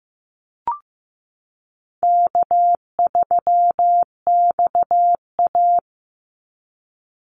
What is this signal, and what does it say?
Morse code, heard as a keyed tone, about 15 wpm, 700 Hz. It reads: K3XA
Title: Easy Call Signs Repeated 2x - 15wpm